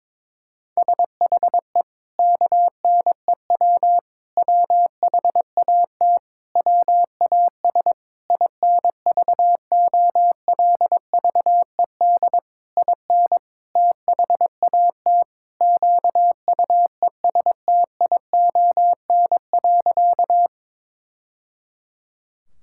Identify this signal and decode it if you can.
Morse code, heard as a keyed tone, about 22 wpm, 700 Hz. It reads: SHE KNEW WHAT WAS INVOLVED IN THAT QUESTION.
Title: Sentences from Top 1000 Words - Review Part 5 - 22wpm